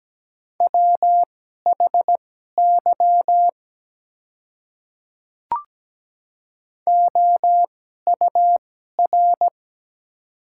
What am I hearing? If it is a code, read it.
Morse code, heard as a keyed tone, about 17 wpm, spaced as for 12 wpm, 690 Hz. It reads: WHY OUR
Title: 2x Character Spacing - Top 300 Words - 17wpm